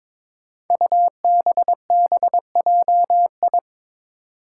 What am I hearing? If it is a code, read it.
Morse code, heard as a keyed tone, about 22 wpm, 700 Hz. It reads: UBBJI
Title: Five Challenging Letter-Number Combinations 22wpm